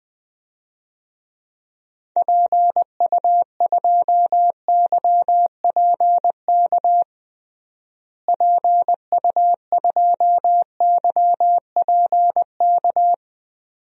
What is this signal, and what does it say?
Morse code, heard as a keyed tone, about 20 wpm, 700 Hz. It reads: PU2YPK PU2YPK
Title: DXCC-100 Call Signs - 6 Characters - Repeated 2x - 20wpm